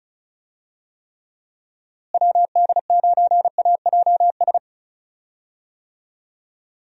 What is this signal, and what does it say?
Morse code, heard as a keyed tone, about 36 wpm, 700 Hz. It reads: WD9AJS
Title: USA Call Signs - 6 Characters - Repeated 3x - 35wpm